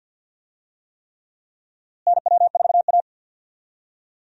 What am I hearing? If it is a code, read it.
Morse code, heard as a keyed tone, about 49 wpm, 700 Hz. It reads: NW4A
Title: USA Call Signs - 4 Characters - Repeated 3x - 50wpm